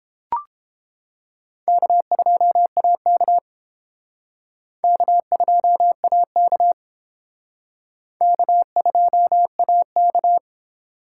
Morse code, heard as a keyed tone, about 28 wpm, 700 Hz. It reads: K2AK K2AK K2AK